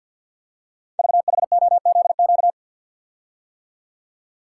Morse code, heard as a keyed tone, about 50 wpm, 690 Hz. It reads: VHOZQ
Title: Code Groups - 5 Letters - 50wpm